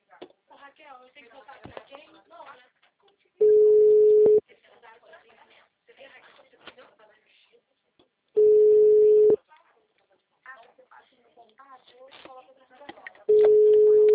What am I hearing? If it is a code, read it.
Ringback tone (Europe, 425 Hz), with speech.
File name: out- -2006- - - .30879